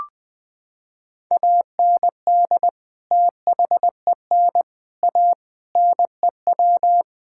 Morse code, heard as a keyed tone, about 20 wpm, 700 Hz. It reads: AND THEN A NEW
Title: Sentences from Top 1000 Words - Review Part 6 - 20wpm